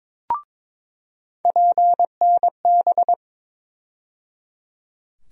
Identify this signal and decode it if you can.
Morse code, heard as a keyed tone, about 21 wpm, 700 Hz. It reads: PNB
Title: USA Call Sign Postfixes - 3 Letters 22wpm